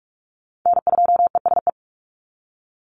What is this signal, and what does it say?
Morse code, heard as a keyed tone, about 44 wpm, 700 Hz. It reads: N2ESE
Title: NA SOTA Chaser Callsigns 45wpm